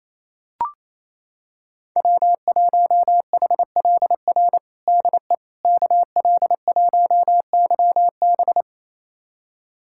Morse code, heard as a keyed tone, about 28 wpm, 700 Hz. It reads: W1HLR DE KL1YB
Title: USA Callsign DE Callsign - 5 Characters - 28wpm